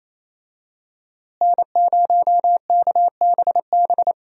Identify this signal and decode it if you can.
Morse code, heard as a keyed tone, about 28 wpm, 700 Hz. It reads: N0KBB
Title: USA Call Signs - 5 Characters - Repeated 2x - 28wpm